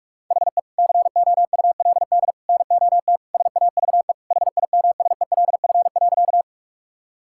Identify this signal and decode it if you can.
Morse code, heard as a keyed tone, about 46 wpm, 700 Hz. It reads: HE COULD NOT SAVE HIMSELF.
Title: Sentences from Top 1600 Words 45wpm